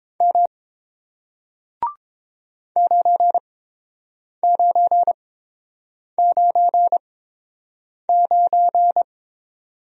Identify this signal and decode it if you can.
Morse code, heard as a keyed tone, about 31 wpm, spaced as for 7 wpm, 690 Hz. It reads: M 9999